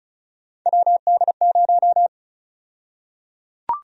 Morse code, heard as a keyed tone, about 36 wpm, 700 Hz. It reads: WD0
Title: USA Call Sign Prefixes - Repeated 2x - 35wpm